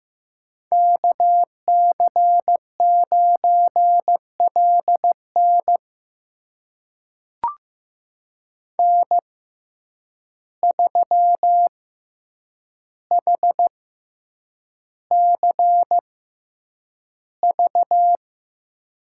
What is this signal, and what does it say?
Morse code, heard as a keyed tone, about 15 wpm, spaced as for 8 wpm, 700 Hz. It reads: KC9LN N3HCV
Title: USA Call Signs - 5 Characters - 6x Character Spacing - 15wpm